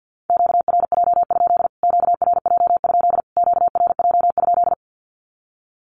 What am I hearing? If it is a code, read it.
Morse code, heard as a keyed tone, about 50 wpm, 700 Hz. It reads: QRJ? QRJ? QRJ?
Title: Q-Codes Repeated 3x 50wpm